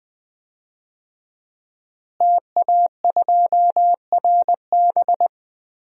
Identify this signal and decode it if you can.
Morse code, heard as a keyed tone, about 20 wpm, 700 Hz. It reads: TA2RB
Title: DXCC-100 Call Signs - 5 Characters - Repeated 2x - 20wpm